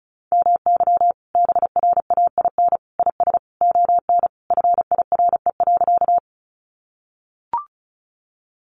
Morse code, heard as a keyed tone, about 34 wpm, 700 Hz. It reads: MY BRAIN IS ON FIRE.
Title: Sentences from Top 1100 Words 35wpm